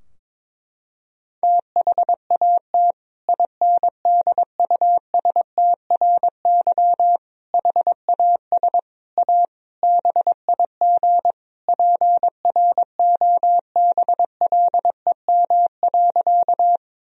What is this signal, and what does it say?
Morse code, heard as a keyed tone, about 21 wpm, 700 Hz. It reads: THAT INDUSTRY HAS A BIG PROBLEM.
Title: Sentences from Top 1000 Words - Review Part 4 - 22wpm